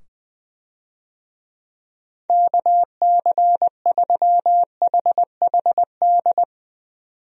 Morse code, heard as a keyed tone, about 21 wpm, 700 Hz. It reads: KC3HHD
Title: USA Call Signs - 6 Characters - 20wpm